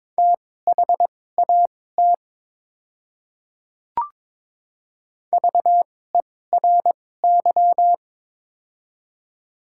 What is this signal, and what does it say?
Morse code, heard as a keyed tone, about 22 wpm, spaced as for 16 wpm, 700 Hz. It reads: THAT VERY
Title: 2x Character Spacing - Top 200 Words - 22wpm